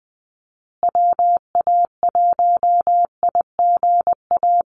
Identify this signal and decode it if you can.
Morse code, heard as a keyed tone, about 20 wpm, 700 Hz. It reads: WA1IGA
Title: USA Call Signs Repeated 3x 20wpm